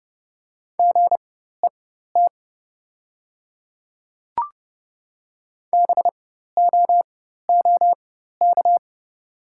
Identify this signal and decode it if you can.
Morse code, heard as a keyed tone, about 30 wpm, spaced as for 14 wpm, 700 Hz. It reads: GET BOOK
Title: 4x Character Spacing - Top 300 Words - 30wpm